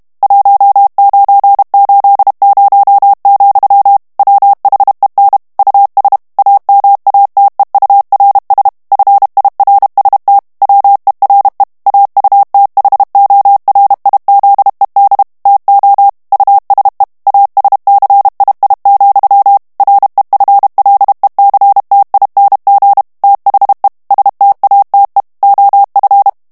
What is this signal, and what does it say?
Morse code, heard as a keyed tone, about 32 wpm, 800 Hz. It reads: 1980, WHEN US AMATEURS FIRST WERE AUTHORIZED TO USE ASCII, REFLECTING THE STATE OF
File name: ANGLAIS texte clair 3 vitesse 30 mots minute